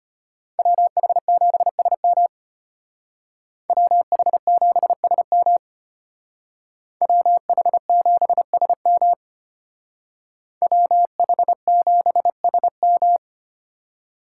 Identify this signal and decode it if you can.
Morse code, heard as a keyed tone, about 32 wpm, 700 Hz. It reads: WH7SM WH7SM WH7SM WH7SM